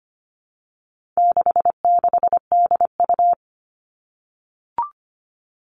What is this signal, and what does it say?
Morse code, heard as a keyed tone, about 25 wpm, 700 Hz. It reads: 66DU
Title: All Cut Numbers - 4-Digit 25wpm